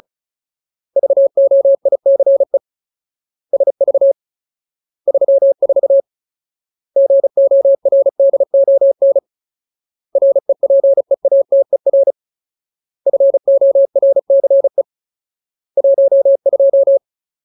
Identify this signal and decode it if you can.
Morse code, heard as a keyed tone, about 34 wpm, 550 Hz. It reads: VOICE SV 34 GORDON REPEATER FORCE 12
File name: Ultimate Word List / 35WPM Ultimate